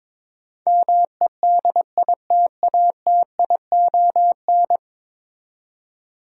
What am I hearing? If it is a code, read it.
Morse code, heard as a keyed tone, about 22 wpm, 700 Hz. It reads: MEDITATION
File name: practice / 22wpm / 10 Letter Words - First 3 Letters Given - 22wpm